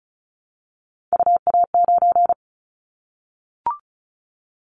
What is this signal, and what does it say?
Morse code, heard as a keyed tone, about 36 wpm, 700 Hz. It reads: UA9
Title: Callsign Prefixes for Top 50 DXCC - 35wpm